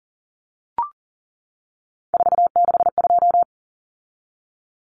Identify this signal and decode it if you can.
Morse code, heard as a keyed tone, about 40 wpm, 700 Hz. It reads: 462
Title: Common Cut Numbers - 3-Digit 40wpm